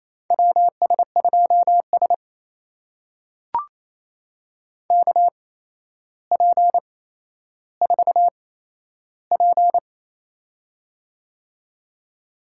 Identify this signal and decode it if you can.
Morse code, heard as a keyed tone, about 28 wpm, spaced as for 12 wpm, 700 Hz. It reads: WS2S KP4P
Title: USA Call Signs - 4 Characters - 8x Character Spacing - 28wpm